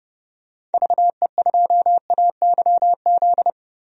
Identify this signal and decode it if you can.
Morse code, heard as a keyed tone, about 30 wpm, 700 Hz. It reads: VE2AYZ
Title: DXCC-100 Call Signs - 6 Characters - Repeated 3x - 30wpm